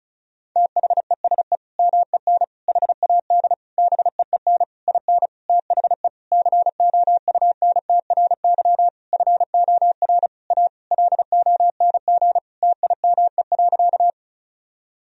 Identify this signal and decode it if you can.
Morse code, heard as a keyed tone, about 36 wpm, 700 Hz. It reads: THESE MEN HAD BEEN IN THE COUNTRY FOR A LONG TIME.
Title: Sentences from Top 1000 Words - Review Part 1 - 35wpm